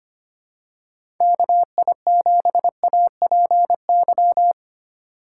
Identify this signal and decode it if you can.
Morse code, heard as a keyed tone, about 25 wpm, 700 Hz. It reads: KI7APY
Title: USA Call Signs - 6 Characters - 2x Character Spacing - 25wpm